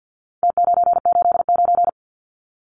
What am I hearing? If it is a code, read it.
Morse code, heard as a keyed tone, about 50 wpm, 700 Hz. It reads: T989
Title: Numbers - Contest Style - 4-Digit 50wpm